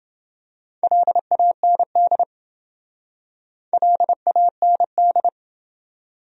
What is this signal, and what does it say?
Morse code, heard as a keyed tone, about 28 wpm, 700 Hz. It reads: LAND LAND